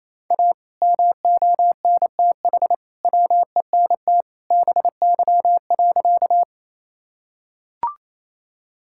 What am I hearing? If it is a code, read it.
Morse code, heard as a keyed tone, about 28 wpm, 700 Hz. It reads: A MONTH WENT BY.
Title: Sentences from Top 1000 Words - Review Part 5 - 28wpm